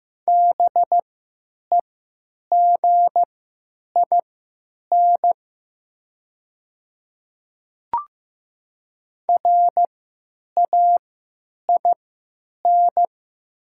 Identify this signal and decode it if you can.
Morse code, heard as a keyed tone, about 15 wpm, spaced as for 9 wpm, 700 Hz. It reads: BEGIN RAIN
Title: 3x Character Spacing - Top 300 Words - 15wpm